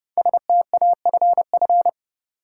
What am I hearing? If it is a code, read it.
Morse code, heard as a keyed tone, about 30 wpm, 700 Hz. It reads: STAFF
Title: Sets of 5 Words Encore - New Word Drill - 30wpm